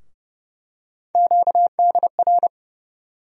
Morse code, heard as a keyed tone, about 30 wpm, 700 Hz. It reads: QDR
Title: USA Call Sign Postfixes - 3 Letters 30wpm